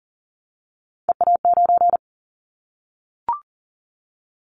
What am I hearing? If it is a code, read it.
Morse code, heard as a keyed tone, about 42 wpm, 700 Hz. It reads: EA9
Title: Callsign Prefixes for Top 100 DXCC - 40wpm